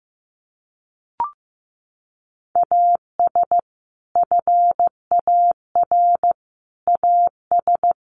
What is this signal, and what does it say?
Morse code, heard as a keyed tone, about 15 wpm, 700 Hz. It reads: AS FAR AS
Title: Sets of 4 Words 15wpm